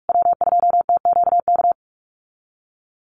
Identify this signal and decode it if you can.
Morse code, heard as a keyed tone, about 44 wpm, 690 Hz. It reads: W2TQK
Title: USA Call Signs Repeated 3x 45wpm